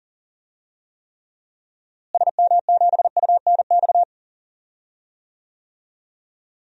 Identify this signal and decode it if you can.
Morse code, heard as a keyed tone, about 40 wpm, 700 Hz. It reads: SM7UNX
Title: DXCC-100 Call Signs - 6 Characters - Repeated 3x - 40wpm